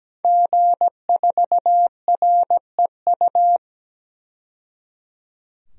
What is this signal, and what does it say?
Morse code, heard as a keyed tone, about 17 wpm, 700 Hz. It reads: G4REU
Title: DXCC-100 Call Signs - 5 Characters - 17wpm